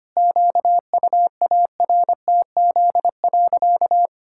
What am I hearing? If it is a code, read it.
Morse code, heard as a keyed tone, about 25 wpm, 700 Hz. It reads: QUARTZ.
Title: Pangram Challenge v2 - 25wpm